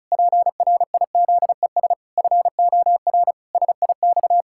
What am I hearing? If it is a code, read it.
Morse code, heard as a keyed tone, about 34 wpm, 700 Hz. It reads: PRIZES FOR SIX
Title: Pangram Challenge v2 - 35wpm